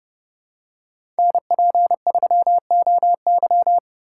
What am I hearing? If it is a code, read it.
Morse code, heard as a keyed tone, about 29 wpm, 700 Hz. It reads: NP3OY
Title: USA Call Signs - 2x Character Spacing - 30wpm